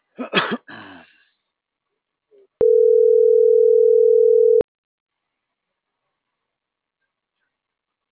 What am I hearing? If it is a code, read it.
Ringback tone (North America), 440 + 480 Hz.